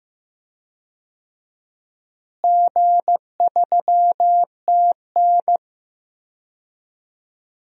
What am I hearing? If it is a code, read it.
Morse code, heard as a keyed tone, about 16 wpm, 700 Hz. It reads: G3TN
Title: DXCC-100 Call Signs - 4 Characters - Repeated 3x - 15wpm